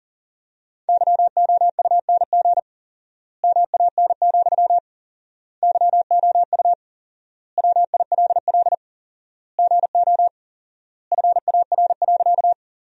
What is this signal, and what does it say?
Morse code, heard as a keyed tone, about 40 wpm, 700 Hz. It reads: YOUNG MAN, YOU WILL GO FAR.
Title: 4x Word-Spacing - Sentences from Top 400 Words - 40wpm